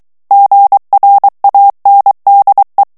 Morse code, heard as a keyed tone, about 23 wpm, 800 Hz. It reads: GRANDE
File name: FRANCAIS TEXTE CLAIR 2 VITESSE 22 MOTS MINUTES